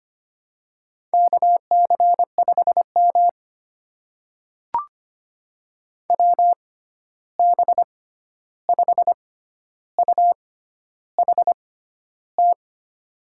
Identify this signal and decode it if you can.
Morse code, heard as a keyed tone, about 25 wpm, spaced as for 11 wpm, 700 Hz. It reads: KC5M WB5UHT
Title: USA Call Signs - 6x Character Spacing - 25wpm